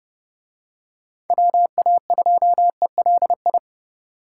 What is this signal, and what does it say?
Morse code, heard as a keyed tone, about 27 wpm, 700 Hz. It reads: WA2ELI